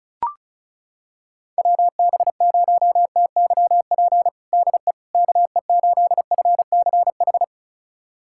Morse code, heard as a keyed tone, about 36 wpm, 690 Hz. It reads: WB0TYP DE KE8FCH
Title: USA Callsign DE Callsign - 6 Characters - 35wpm